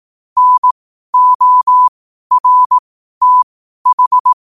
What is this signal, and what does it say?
Morse code, heard as a keyed tone, about 18 wpm, spaced as for 12 wpm, 1000 Hz. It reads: NORTH
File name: qotdfiles / 12wpm